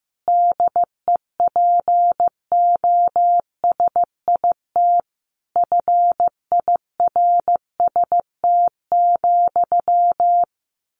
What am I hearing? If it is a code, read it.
Morse code, heard as a keyed tone, about 15 wpm, 700 Hz. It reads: DEPOSIT FIRST,
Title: Sentences from Top 1300 Words 15wpm